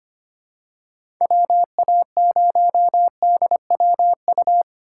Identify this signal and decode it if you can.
Morse code, heard as a keyed tone, about 25 wpm, 700 Hz. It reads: WA0DWU
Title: USA Call Signs - 6 Characters - 2x Character Spacing - 25wpm